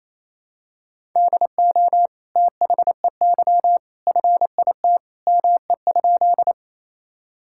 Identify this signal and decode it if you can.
Morse code, heard as a keyed tone, about 28 wpm, 700 Hz. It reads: DO THEY FIT ME?
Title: Sentences from Top 1000 Words - Review Part 5 - 28wpm